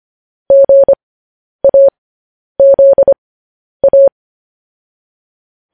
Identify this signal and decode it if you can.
Morse code, heard as a keyed tone, about 25 wpm, spaced as for 10 wpm, 550 Hz. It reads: GAZA